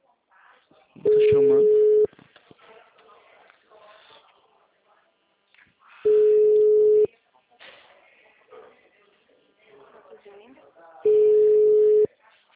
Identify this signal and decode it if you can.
Ringback tone (Europe, 425 Hz), with speech.
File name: out- -2005- - - .4